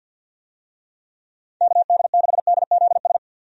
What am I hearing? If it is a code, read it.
Morse code, heard as a keyed tone, about 51 wpm, 700 Hz. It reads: KD6DZS
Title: USA Call Signs - 6 Characters - 50wpm